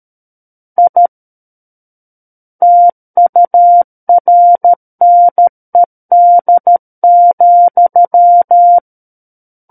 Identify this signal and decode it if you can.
Morse code, heard as a keyed tone, about 13 wpm, 700 Hz. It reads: I TURNED,
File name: Book One / 04-Chapter 4 - 13 WPM